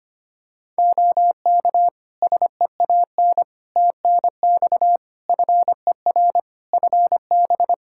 Morse code, heard as a keyed tone, about 25 wpm, 700 Hz. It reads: OK SEAN TNX FER FB
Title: Practice QSO - Exchange 4 - Ending - 25wpm